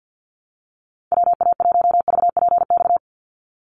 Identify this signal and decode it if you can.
Morse code, heard as a keyed tone, about 48 wpm, 700 Hz. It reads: WA1VPX
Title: USA Call Signs 50wpm